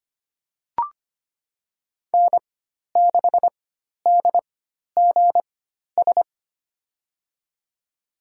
Morse code, heard as a keyed tone, about 25 wpm, spaced as for 12 wpm, 700 Hz. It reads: N6DGS
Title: USA Call Signs - 4x Character Spacing - 25wpm